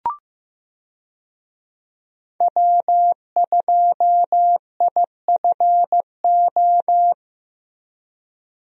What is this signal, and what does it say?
Morse code, heard as a keyed tone, about 16 wpm, 700 Hz. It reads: W2IFO
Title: USA Call Signs - 5 Characters - Repeated 2x - 15wpm